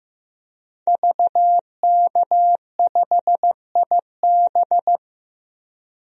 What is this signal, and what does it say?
Morse code, heard as a keyed tone, about 15 wpm, 700 Hz. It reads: VK5IB
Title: DXCC-100 Call Signs - 5 Characters - 15wpm